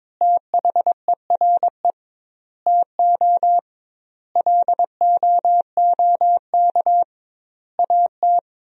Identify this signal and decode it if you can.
Morse code, heard as a keyed tone, about 22 wpm, 700 Hz. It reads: THERE TO LOOK AT
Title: 2x Word-Spacing - Sentences from Top 100 Words - 22wpm